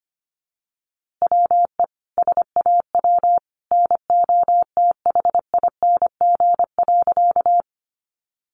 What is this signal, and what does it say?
Morse code, heard as a keyed tone, about 25 wpm, 700 Hz. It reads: WE SAW NOTHING.
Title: A Sound Of Thunder 25wpm